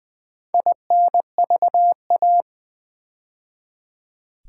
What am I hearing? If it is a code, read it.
Morse code, heard as a keyed tone, about 20 wpm, 700 Hz. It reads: INVA
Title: Build A Word - 2 Letters At A Time - 6-Letter Words - 20wpm